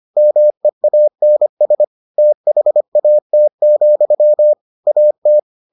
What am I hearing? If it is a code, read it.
Morse code, heard as a keyed tone, about 25 wpm, 600 Hz. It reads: MEANS THAT, AT